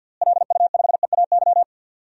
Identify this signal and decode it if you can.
Morse code, heard as a keyed tone, about 51 wpm, 700 Hz. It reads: LU5EAY
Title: DXCC-100 Call Signs - 6 Characters - Repeated 3x - 50wpm